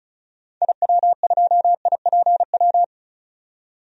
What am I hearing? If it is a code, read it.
Morse code, heard as a keyed tone, about 36 wpm, 700 Hz. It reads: IW2IPW
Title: DXCC-100 Call Signs - 6 Characters - 35wpm